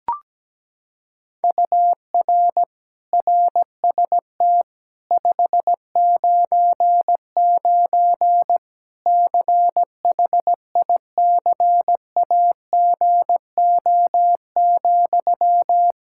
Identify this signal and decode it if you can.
Morse code, heard as a keyed tone, about 17 wpm, 700 Hz. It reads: UR RST 599 CHICAGO,
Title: Practice QSO - Exchange 1 - Beginning - 17wpm